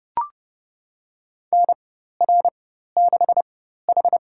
Morse code, heard as a keyed tone, about 30 wpm, spaced as for 14 wpm, 700 Hz. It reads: NR6H
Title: USA Call Signs - 4 Characters - 4x Character Spacing - 30wpm